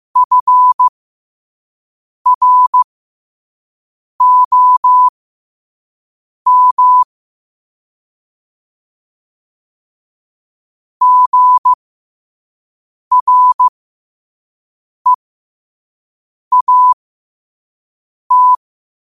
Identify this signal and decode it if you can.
Morse code, heard as a keyed tone, about 15 wpm, spaced as for 5 wpm, 1000 Hz. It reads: FROM GREAT
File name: qotdfiles / 5wpm